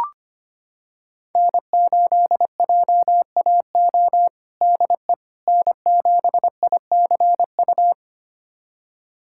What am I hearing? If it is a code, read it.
Morse code, heard as a keyed tone, about 25 wpm, 700 Hz. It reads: N8JAO DE N7ICU
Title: USA Callsign DE Callsign 25wpm